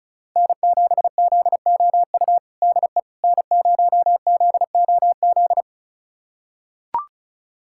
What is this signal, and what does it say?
Morse code, heard as a keyed tone, about 34 wpm, 690 Hz. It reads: N7ZOU DE N0ZOZ
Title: USA Callsign DE Callsign - 5 Characters - 35wpm